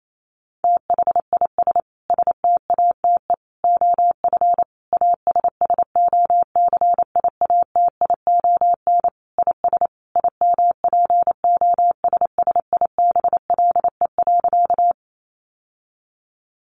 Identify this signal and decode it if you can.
Morse code, heard as a keyed tone, about 28 wpm, 700 Hz. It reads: THIS STATE OF ASSOCIATION IS IMPOSSIBLE.
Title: Sentences from Top 1100 Words 28wpm